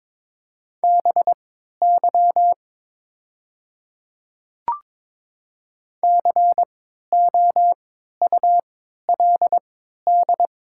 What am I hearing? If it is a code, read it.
Morse code, heard as a keyed tone, about 22 wpm, spaced as for 13 wpm, 700 Hz. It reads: BY COULD
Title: 3x Character Spacing - Top 100 Words - 22wpm